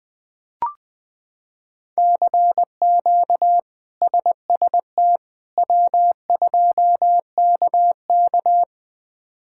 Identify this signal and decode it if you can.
Morse code, heard as a keyed tone, about 21 wpm, 700 Hz. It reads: CQ SST W2KK
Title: K1USN SST QSO Practice - 20wpm